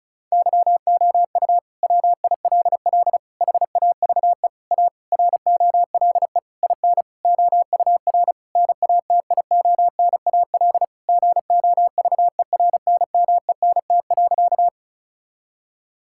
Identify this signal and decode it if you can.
Morse code, heard as a keyed tone, about 34 wpm, 700 Hz. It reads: YOU WILL HAVE A ROLE IN OUR NATIONAL GOVERNMENT.
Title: Sentences from Top 1000 Words - Review Part 6 - 35wpm